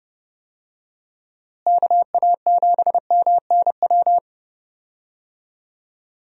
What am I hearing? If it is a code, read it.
Morse code, heard as a keyed tone, about 31 wpm, 700 Hz. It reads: KA7MNW
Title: USA Call Signs - 6 Characters - Repeated 3x - 30wpm